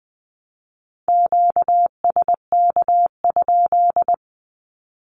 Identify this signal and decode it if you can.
Morse code, heard as a keyed tone, about 20 wpm, 700 Hz. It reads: QSK?
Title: Q-Codes 20wpm